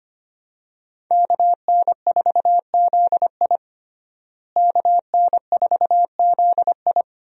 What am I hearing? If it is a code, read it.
Morse code, heard as a keyed tone, about 25 wpm, 700 Hz. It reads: KN4ZI KN4ZI
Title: USA Call Signs - 5 Characters - Repeated 2x - 25wpm